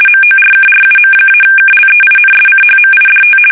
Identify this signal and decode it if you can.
RTTY, about 50 baud. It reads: MORKRUM-KLEINSCHMIDT
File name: fsk850-50B / teletype-corp